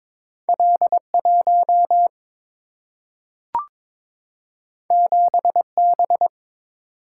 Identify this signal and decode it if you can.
Morse code, heard as a keyed tone, about 22 wpm, 700 Hz. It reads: L1 7B
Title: Single-Letter Single-Number 22wpm